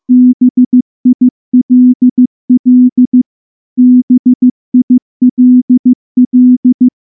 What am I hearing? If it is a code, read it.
Morse code, heard as a keyed tone, about 14 wpm, 250 Hz. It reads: BILL BILL
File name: morse / morse-training-15WPM